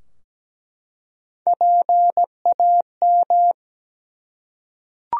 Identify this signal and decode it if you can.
Morse code, heard as a keyed tone, about 17 wpm, 700 Hz. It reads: PAM
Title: USA Common Names - Max 4 Letters - 17wpm